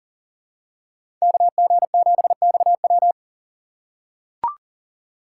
Morse code, heard as a keyed tone, about 38 wpm, 700 Hz. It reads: KG7XW
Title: USA Call Signs - 5 Characters - 2x Character Spacing - 40wpm